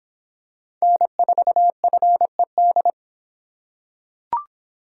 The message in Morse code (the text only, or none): N4FED